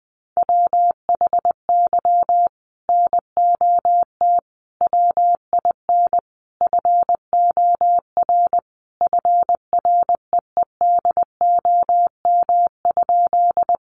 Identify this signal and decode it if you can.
Morse code, heard as a keyed tone, about 20 wpm, 700 Hz. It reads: WHY NOT WIN FOR FREEDOM?
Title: Sentences from Top 1500 Words 20wpm